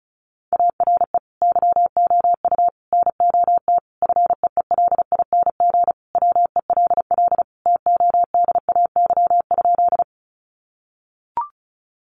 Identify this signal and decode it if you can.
Morse code, heard as a keyed tone, about 34 wpm, 700 Hz. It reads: ARE YOU NOT FEELING WELL TODAY?
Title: Sentences from Top 1200 Words 35wpm